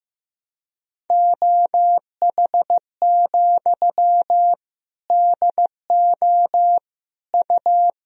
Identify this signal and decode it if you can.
Morse code, heard as a keyed tone, about 15 wpm, 700 Hz. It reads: OH, DO U
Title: Sentences from Top 500 Words and Abbreviations 15wpm